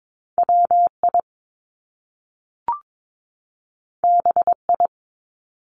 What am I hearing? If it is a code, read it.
Morse code, heard as a keyed tone, about 22 wpm, 700 Hz. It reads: WI BI
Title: Top Two Letter Combinations in Words 22wpm